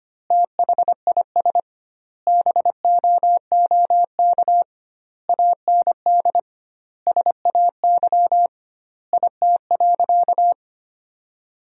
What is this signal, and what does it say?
Morse code, heard as a keyed tone, about 25 wpm, 700 Hz. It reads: THIS BOOK AND SAY IT.
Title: 2x Word-Spacing - Sentences from Top 400 Words - 25wpm